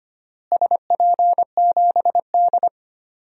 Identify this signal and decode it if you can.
Morse code, heard as a keyed tone, about 25 wpm, 700 Hz. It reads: SP7D
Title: DXCC-100 Call Signs - 4 Characters - Repeated 3x - 25wpm